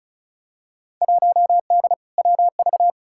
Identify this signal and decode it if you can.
Morse code, heard as a keyed tone, about 34 wpm, 700 Hz. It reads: 1D WV
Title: Field Day Exchanges - 35wpm